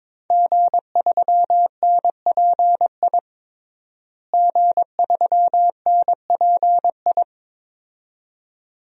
Morse code, heard as a keyed tone, about 22 wpm, 700 Hz. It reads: G3NPI G3NPI
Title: DXCC-100 Call Signs - 5 Characters - Repeated 2x - 22wpm